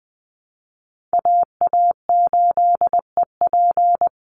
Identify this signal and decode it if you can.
Morse code, heard as a keyed tone, about 20 wpm, 700 Hz. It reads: AA8EP
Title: USA Call Signs Repeated 3x 20wpm